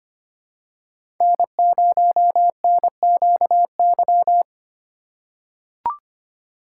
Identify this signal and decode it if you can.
Morse code, heard as a keyed tone, about 25 wpm, 700 Hz. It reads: N0NQY
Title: USA Call Signs - 5 Characters - 4x Character Spacing - 25wpm